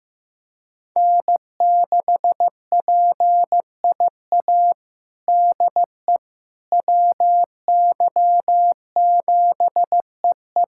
Morse code, heard as a keyed tone, about 15 wpm, 700 Hz. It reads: N6PIA DE WY7EE
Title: USA Callsign DE Callsign - 5 Characters - 15wpm